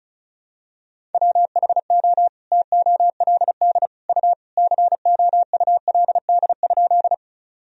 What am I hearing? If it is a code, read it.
Morse code, heard as a keyed tone, about 36 wpm, 700 Hz. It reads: WHO TOLD U COULD?
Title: Sentences from Top 400 Words and Abbreviations 35wpm